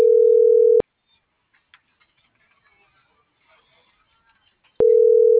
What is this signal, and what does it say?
Ringback tone (North America), 440 + 480 Hz.